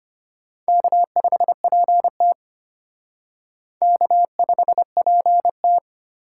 Morse code, heard as a keyed tone, about 32 wpm, 700 Hz. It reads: K5PT K5PT